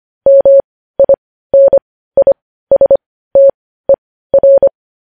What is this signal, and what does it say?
Morse code, heard as a keyed tone, about 25 wpm, spaced as for 15 wpm, 550 Hz. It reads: MINISTER
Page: news headlines in morse code at 15 wpm